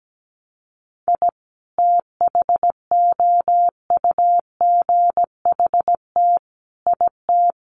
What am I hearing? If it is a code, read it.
Morse code, heard as a keyed tone, about 17 wpm, 700 Hz. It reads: I THOUGHT IT
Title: Sets of 5 Words 17wpm